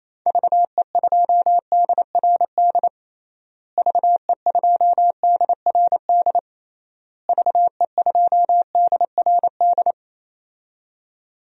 Morse code, heard as a keyed tone, about 28 wpm, 700 Hz. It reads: VE2DRD VE2DRD VE2DRD
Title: DXCC-100 Call Signs - 6 Characters - Repeated 3x - 28wpm